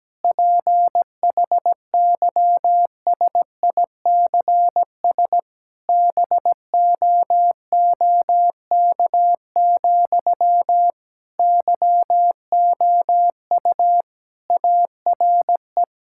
Morse code, heard as a keyed tone, about 17 wpm, 700 Hz. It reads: PHYSICS BOOK, YOU ARE
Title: Sentences from Top 1000 Words - Review Part 5 - 17wpm